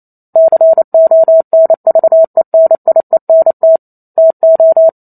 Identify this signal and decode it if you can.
Morse code, heard as a keyed tone, about 28 wpm, 650 Hz. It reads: CONVENIENT TO